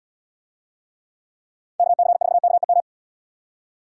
Morse code, heard as a keyed tone, about 76 wpm, 700 Hz. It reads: KC4KEM